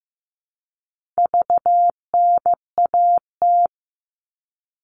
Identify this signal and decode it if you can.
Morse code, heard as a keyed tone, about 15 wpm, 700 Hz. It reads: VNAT
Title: All Cut Numbers - 4-Digit 15wpm